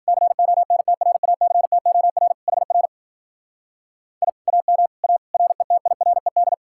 Morse code, heard as a keyed tone, about 48 wpm, 700 Hz. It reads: CONTRACTOR HR I AM A RETIRED
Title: Speed-Racing Common QSO Phrases Encore - 40wpm